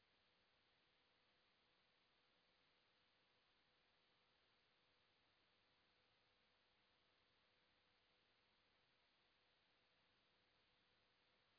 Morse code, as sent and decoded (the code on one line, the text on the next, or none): none